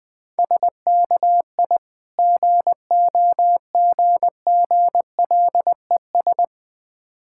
Morse code, heard as a keyed tone, about 20 wpm, 690 Hz. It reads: SKI GOGGLES
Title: Difficult Words with Hints - 20wpm